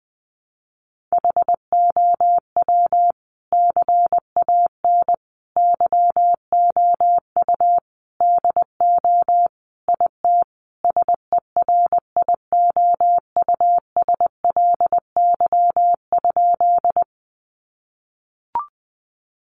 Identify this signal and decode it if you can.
Morse code, heard as a keyed tone, about 19 wpm, 700 Hz. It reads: HOW CAN YOU DO IT SERIOUSLY?
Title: Sentences from Top 1200 Words 20wpm